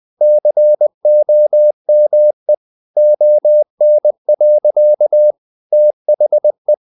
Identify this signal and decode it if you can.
Morse code, heard as a keyed tone, about 20 wpm, 600 Hz. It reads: COME ON. THE
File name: the old man and the sea / the old man and the sea part 1 20wpm